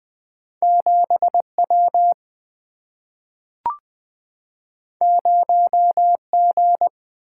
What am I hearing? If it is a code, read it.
Morse code, heard as a keyed tone, about 20 wpm, 700 Hz. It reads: 7 W 0 G
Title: Single-Letter Single-Number 20wpm